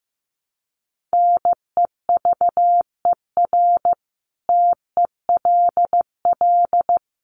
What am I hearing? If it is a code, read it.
Morse code, heard as a keyed tone, about 15 wpm, 700 Hz. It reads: NEVER TELL
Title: Sentences from Top 900 Words 15wpm